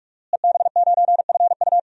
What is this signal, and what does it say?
Morse code, heard as a keyed tone, about 44 wpm, 700 Hz. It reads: EB9FU
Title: DXCC-100 Call Signs - 5 Characters - Repeated 3x - 45wpm